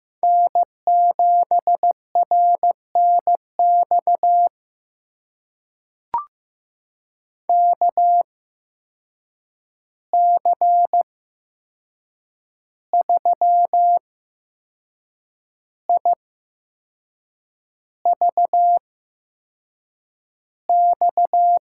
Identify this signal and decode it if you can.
Morse code, heard as a keyed tone, about 15 wpm, spaced as for 6 wpm, 700 Hz. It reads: N7RNX KC3IVX
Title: USA Call Signs - 8x Character Spacing - 15wpm